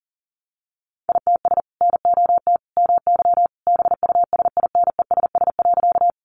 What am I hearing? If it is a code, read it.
Morse code, heard as a keyed tone, about 40 wpm, 700 Hz. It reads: ITS NOT MY BUSINESS.
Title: Sentences from Top 600 Words 40wpm